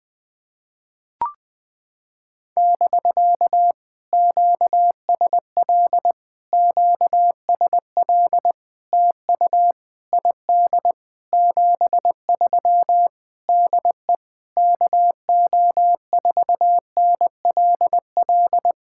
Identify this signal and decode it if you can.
Morse code, heard as a keyed tone, about 20 wpm, 700 Hz. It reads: <BK> QSL QSL TU ID 73 DE KO4NLL
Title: POTA QSO Practice - 20wpm